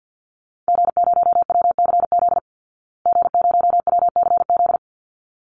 Morse code, heard as a keyed tone, about 50 wpm, 700 Hz. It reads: G0WCZ G0WCZ
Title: Call Signs Repeated 3x - 50wpm